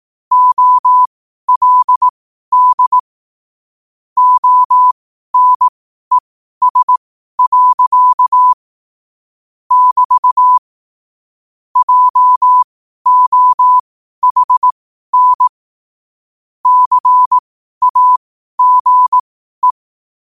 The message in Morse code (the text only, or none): OLD ONES. <BT> JOHN CAGE